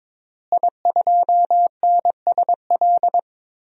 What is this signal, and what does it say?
Morse code, heard as a keyed tone, about 21 wpm, 700 Hz. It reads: I2NSL
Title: DXCC-100 Call Signs - 5 Characters - Repeated 2x - 22wpm